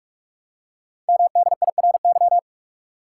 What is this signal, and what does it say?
Morse code, heard as a keyed tone, about 44 wpm, 700 Hz. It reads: MDIRY
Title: Code Groups - 5 Letters - 45wpm